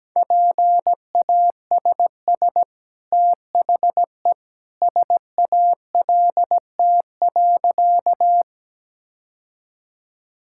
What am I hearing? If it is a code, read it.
Morse code, heard as a keyed tone, about 17 wpm, 700 Hz. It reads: PASS THE SALT.
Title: Sentences from Top 1000 Words - Review Part 5 - 17wpm